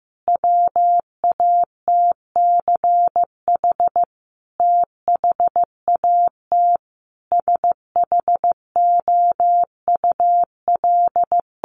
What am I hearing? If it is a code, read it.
Morse code, heard as a keyed tone, about 15 wpm, 700 Hz. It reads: WATCH THAT SHOUL
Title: Sentences from Top 1000 Words 15wpm